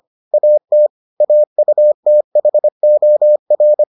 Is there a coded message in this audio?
Yes: Morse code from 0.3 to 3.8 s, heard as a keyed tone, about 25 wpm, 600 Hz. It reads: AT AUTHOR